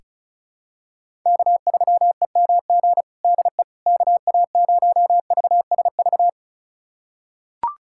Morse code, heard as a keyed tone, about 36 wpm, 700 Hz. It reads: K3EMG DE KA0VSV
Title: USA Callsign DE Callsign 35wpm